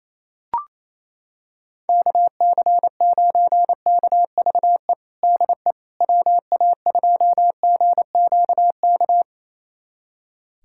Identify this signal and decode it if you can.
Morse code, heard as a keyed tone, about 28 wpm, 700 Hz. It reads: KC9KVE DE WA2GQK
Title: USA Callsign DE Callsign - 6 Characters - 28wpm